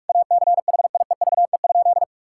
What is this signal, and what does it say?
Morse code, heard as a keyed tone, about 45 wpm, 700 Hz. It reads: ACHIEVE?
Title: Sentences from Top 1000 Words - Review Part 6 - 45wpm